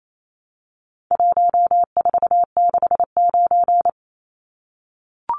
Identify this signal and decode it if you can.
Morse code, heard as a keyed tone, about 28 wpm, 700 Hz. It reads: 1469
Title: Numbers - 4-Digit 28wpm